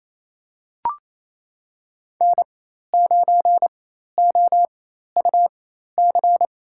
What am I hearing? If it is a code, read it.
Morse code, heard as a keyed tone, about 28 wpm, spaced as for 13 wpm, 700 Hz. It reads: N9OUC
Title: USA Call Signs - 5 Characters - 4x Character Spacing - 28wpm